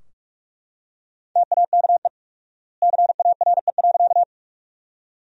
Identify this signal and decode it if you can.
Morse code, heard as a keyed tone, about 43 wpm, 700 Hz. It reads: TAKE CARE.
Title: 4x Word-Spacing - Sentences from Top 300 Words - 45wpm